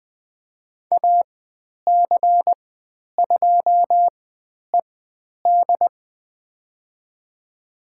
Morse code, heard as a keyed tone, about 20 wpm, spaced as for 10 wpm, 700 Hz. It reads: AC2ED
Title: 102 Lesson 17g - Letter u - Call Signs 20wpm